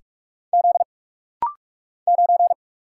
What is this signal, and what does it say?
Morse code, heard as a keyed tone, about 44 wpm, 700 Hz. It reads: Z 9
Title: Single Letter-Number Alt - Rapid-Fire 45wpm